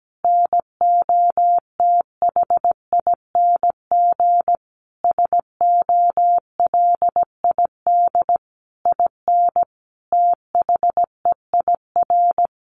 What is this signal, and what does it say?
Morse code, heard as a keyed tone, about 17 wpm, 700 Hz. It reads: NOTHING SOLID IN THEIR
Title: Sentences from Top 1500 Words 17wpm